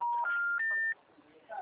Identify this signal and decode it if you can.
Special information tone.